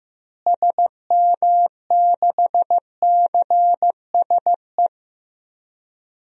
Morse code, heard as a keyed tone, about 15 wpm, 700 Hz. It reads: SM6CSE
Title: DXCC-100 Call Signs - 6 Characters - Repeated 2x - 15wpm